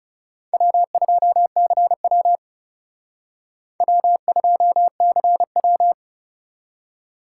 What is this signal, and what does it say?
Morse code, heard as a keyed tone, about 38 wpm, 700 Hz. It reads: W2CW W2CW